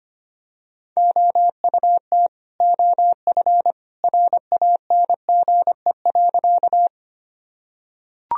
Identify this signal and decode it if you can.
Morse code, heard as a keyed tone, about 25 wpm, 700 Hz. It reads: OUT OF RANGE.
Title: Sentences from Top 1000 Words - Review Part 4 - 25wpm